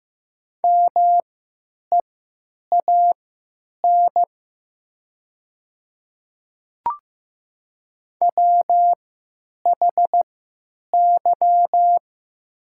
Morse code, heard as a keyed tone, about 15 wpm, spaced as for 9 wpm, 700 Hz. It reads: MEAN WHY
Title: 3x Character Spacing - Top 200 Words - 15wpm